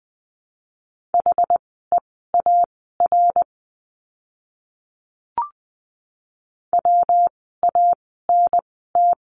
Morse code, heard as a keyed tone, about 20 wpm, spaced as for 14 wpm, 700 Hz. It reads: HEAR WANT
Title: 206 Lesson 33f - Letter j - Words 20wpm